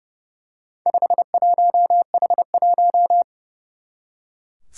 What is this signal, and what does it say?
Morse code, heard as a keyed tone, about 30 wpm, 700 Hz. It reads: 51H1
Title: Four Challenging Letter-Number Combinations 30wpm